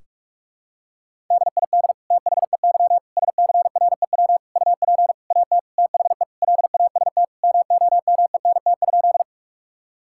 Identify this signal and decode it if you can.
Morse code, heard as a keyed tone, about 46 wpm, 700 Hz. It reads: DID THEY SCREW UP AT THE LAST MOMENT?
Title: Sentences from Top 1600 Words 45wpm